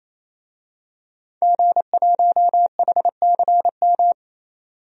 Morse code, heard as a keyed tone, about 28 wpm, 700 Hz. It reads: G1HCM
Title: DXCC-100 Call Signs - 5 Characters - 28wpm